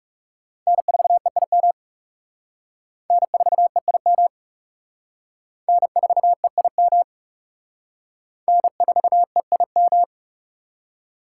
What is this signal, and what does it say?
Morse code, heard as a keyed tone, about 37 wpm, 700 Hz. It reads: N4EIM N4EIM N4EIM N4EIM
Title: Speed Racing - USA Call Signs 30wpm